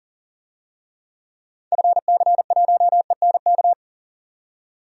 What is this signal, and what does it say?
Morse code, heard as a keyed tone, about 40 wpm, 700 Hz. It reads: FC1ENK
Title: DXCC-100 Call Signs - 6 Characters - 40wpm